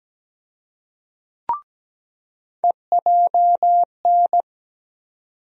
Morse code, heard as a keyed tone, about 17 wpm, 700 Hz. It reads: EJN
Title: USA Call Sign Postfixes - 3 Letters 17wpm